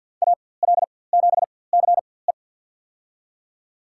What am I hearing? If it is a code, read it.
Morse code, heard as a keyed tone, about 53 wpm, spaced as for 24 wpm, 700 Hz. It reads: AL7CE
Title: USA Call Signs - 4x Character Spacing - 50wpm